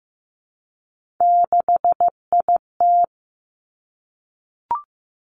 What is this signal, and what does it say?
Morse code, heard as a keyed tone, about 15 wpm, 700 Hz. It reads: 6IT
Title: Three Challenging Letter-Number Combinations 15wpm